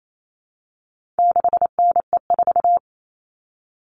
Morse code, heard as a keyed tone, about 28 wpm, 700 Hz. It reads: 6NE4
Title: All Cut Numbers - 4-Digit 28wpm